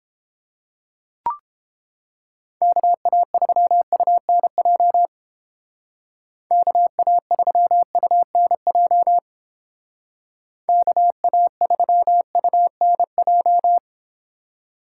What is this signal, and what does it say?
Morse code, heard as a keyed tone, about 28 wpm, 700 Hz. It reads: KA3UNJ KA3UNJ KA3UNJ